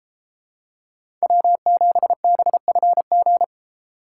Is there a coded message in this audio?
Yes: Morse code from 1.2 to 3.4 s, heard as a keyed tone, about 31 wpm, 700 Hz. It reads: W7BFG